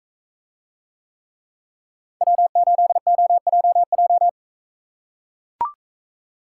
Morse code, heard as a keyed tone, about 44 wpm, 700 Hz. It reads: W8OJJ